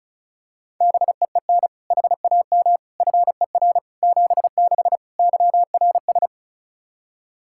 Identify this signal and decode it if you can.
Morse code, heard as a keyed tone, about 36 wpm, 700 Hz. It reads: BEEN HAM FER 76 YRS
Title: Practice QSO - Exchange 3 - More Info - 35wpm